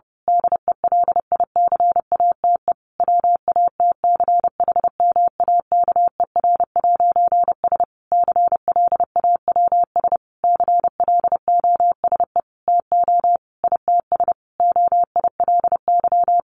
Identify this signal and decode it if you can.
Morse code, heard as a keyed tone, about 30 wpm, 700 Hz. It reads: DELICATE WATCHMAKER'S CLAWS CLOSE TO ITS OILY
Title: A Sound Of Thunder 30wpm